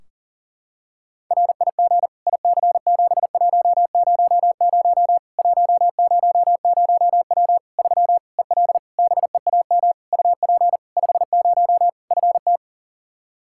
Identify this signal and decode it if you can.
Morse code, heard as a keyed tone, about 37 wpm, 700 Hz. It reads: RIG IC7100 100W 3 EL BEAM UP 50 FT
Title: Practice QSO - Exchange 2 - Info - 40wpm